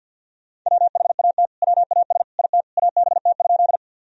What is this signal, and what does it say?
Morse code, heard as a keyed tone, about 50 wpm, 700 Hz. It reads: WHAT WAS IT ABT?
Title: Sentences from Top 100 Words and Abbreviations 50wpm